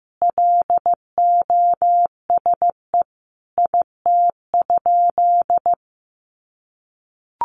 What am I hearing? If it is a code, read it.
Morse code, heard as a keyed tone, about 15 wpm, 700 Hz. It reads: LOSE IT?
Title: Sentences from Top 1100 Words 15wpm